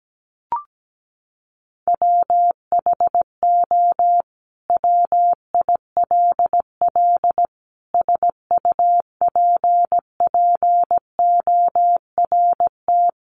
Morse code, heard as a keyed tone, about 17 wpm, 700 Hz. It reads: WHO WILL SUPPORT
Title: Sentences from Top 800 Words 17wpm